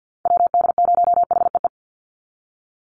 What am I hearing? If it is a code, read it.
Morse code, heard as a keyed tone, about 50 wpm, 700 Hz. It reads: WD0HEE
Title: USA Call Signs 50wpm